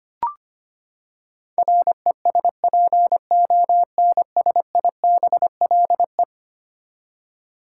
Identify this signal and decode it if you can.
Morse code, heard as a keyed tone, about 25 wpm, 700 Hz. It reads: RESPONSIBLE
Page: morse-code-ninja-practice-25wpm